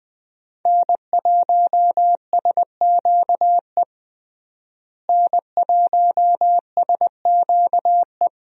Morse code, heard as a keyed tone, about 21 wpm, 700 Hz. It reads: N1SQE N1SQE
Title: USA Call Signs - 5 Characters - Repeated 2x - 20wpm